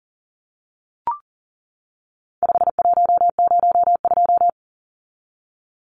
Morse code, heard as a keyed tone, about 40 wpm, 700 Hz. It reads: 5102
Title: Numbers - 4-Digit 40wpm